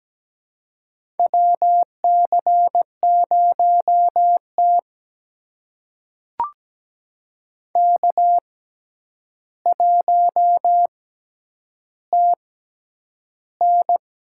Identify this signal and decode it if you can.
Morse code, heard as a keyed tone, about 17 wpm, spaced as for 9 wpm, 700 Hz. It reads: WC0T K1TN
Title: USA Call Signs - 6x Character Spacing - 17wpm